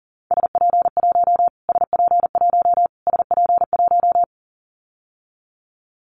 Morse code, heard as a keyed tone, about 40 wpm, 700 Hz. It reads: SP1 SP1 SP1
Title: Callsign Prefixes for Top 25 DXCC - Repeated 3x - 40wpm